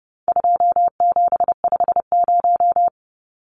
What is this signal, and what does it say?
Morse code, heard as a keyed tone, about 30 wpm, 700 Hz. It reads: 2750
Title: Numbers - 4-Digit 30wpm